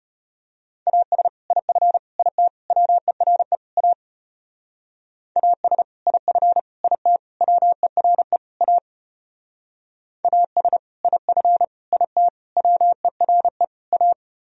Morse code, heard as a keyed tone, about 30 wpm, 700 Hz. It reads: AS IF IT WERE A AS IF IT WERE A AS IF IT WERE A
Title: Speed-Racing - Sets of 5 Words 25wpm